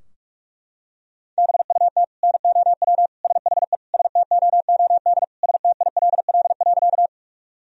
Morse code, heard as a keyed tone, about 41 wpm, 700 Hz. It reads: BUT NOW SHE STOOD STILL.
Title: Sentences from Top 1000 Words - Review Part 2 - 45wpm